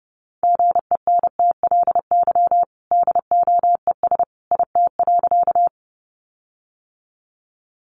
Morse code, heard as a keyed tone, about 30 wpm, 700 Hz. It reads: GENTLY DOES IT.
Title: Sentences from Top 1500 Words 30wpm